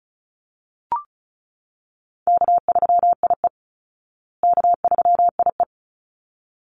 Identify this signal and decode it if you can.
Morse code, heard as a keyed tone, about 34 wpm, 700 Hz. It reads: K3IE K3IE
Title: Easy Call Signs Repeated 2x - 35wpm